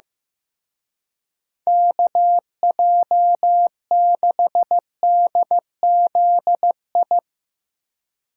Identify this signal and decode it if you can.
Morse code, heard as a keyed tone, about 16 wpm, 700 Hz. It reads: KJ6DZI
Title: Speed Racing - USA Call Signs 15wpm